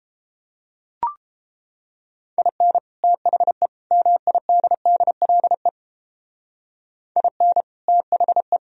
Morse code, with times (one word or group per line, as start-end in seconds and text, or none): 2.4-2.8 IN
3.0-3.7 THE
3.9-5.7 MIDDLE
7.2-7.6 IN
7.9-8.6 THE